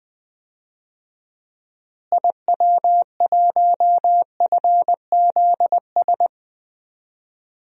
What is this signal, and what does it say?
Morse code, heard as a keyed tone, about 20 wpm, 700 Hz. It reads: IW1FZS
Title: DXCC-100 Call Signs - 6 Characters - Repeated 2x - 20wpm